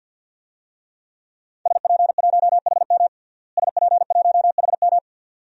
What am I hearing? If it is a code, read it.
Morse code, heard as a keyed tone, about 50 wpm, 700 Hz. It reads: SP1HM SP1HM
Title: DXCC-100 Call Signs - 5 Characters - Repeated 2x - 50wpm